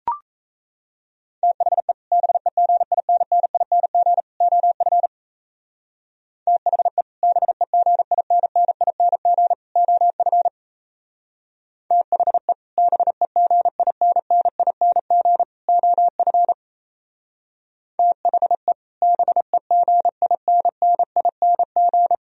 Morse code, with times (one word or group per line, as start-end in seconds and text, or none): 1.4-1.9 THE
2.1-4.2 BEGINNING
4.4-5.1 OF
6.5-7.0 THE
7.2-9.5 BEGINNING
9.8-10.5 OF
11.9-12.5 THE
12.8-15.4 BEGINNING
15.7-16.5 OF
18.0-18.7 THE
19.0-22.1 BEGINNING